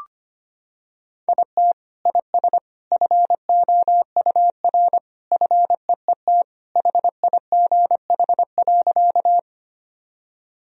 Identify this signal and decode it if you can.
Morse code, heard as a keyed tone, about 25 wpm, 700 Hz. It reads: IT IS FOUR FEET HIGH.
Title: Sentences from Top 1000 Words - Review Part 1 - 25wpm